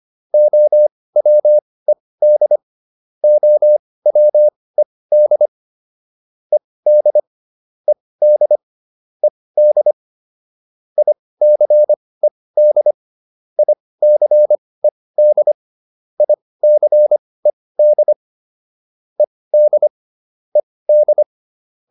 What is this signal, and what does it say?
Morse code, heard as a keyed tone, about 25 wpm, spaced as for 18 wpm, 600 Hz. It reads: OWED OWED ED ED ED ICED ICED ICED ED ED